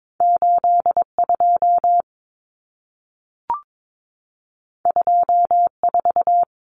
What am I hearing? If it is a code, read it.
Morse code, heard as a keyed tone, about 22 wpm, 700 Hz. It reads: 82 24
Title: Common Cut Numbers - 2-Digit 22wpm